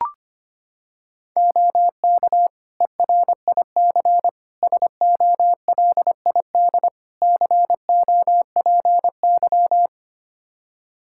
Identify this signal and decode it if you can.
Morse code, heard as a keyed tone, about 25 wpm, 700 Hz. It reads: OK ERIC SOLID COPY
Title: Practice QSO - Exchange 3 - More Info - 25wpm